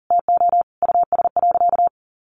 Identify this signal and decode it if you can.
Morse code, heard as a keyed tone, about 40 wpm, 700 Hz. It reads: TO US.
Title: Sentences from Top 1100 Words 40wpm